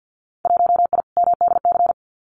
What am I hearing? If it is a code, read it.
Morse code, heard as a keyed tone, about 50 wpm, 700 Hz. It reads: 1I MDC
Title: Winter Field Day Exchanges 50wpm